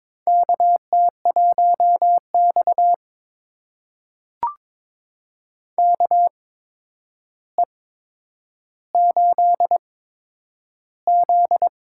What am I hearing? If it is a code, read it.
Morse code, heard as a keyed tone, about 22 wpm, spaced as for 9 wpm, 700 Hz. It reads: KT1X KE8Z
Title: USA Call Signs - 4 Characters - 8x Character Spacing - 22wpm